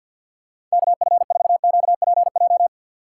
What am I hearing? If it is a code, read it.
Morse code, heard as a keyed tone, about 52 wpm, 700 Hz. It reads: KF4QPJ
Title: USA Call Signs - 2x Character Spacing - 50wpm